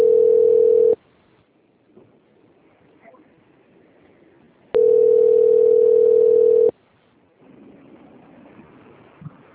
Ringback tone (North America), 440 + 480 Hz.
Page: Piracicaba - SP